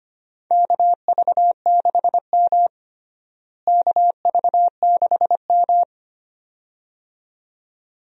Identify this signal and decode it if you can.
Morse code, heard as a keyed tone, about 25 wpm, 700 Hz. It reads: KV6M KV6M
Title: USA Call Signs - 4 Characters - Repeated 2x - 25wpm